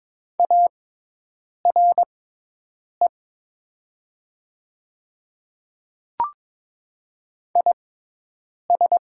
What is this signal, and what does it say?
Morse code, heard as a keyed tone, about 22 wpm, spaced as for 8 wpm, 700 Hz. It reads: ARE IS
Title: 6x Character Spacing - Top 100 Words - 22wpm